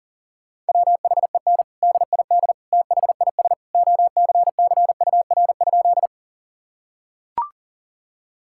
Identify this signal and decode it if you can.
Morse code, heard as a keyed tone, about 40 wpm, 700 Hz. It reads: WHEN DID THIS OCCUR?
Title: Sentences from Top 1000 Words - Review Part 6 - 40wpm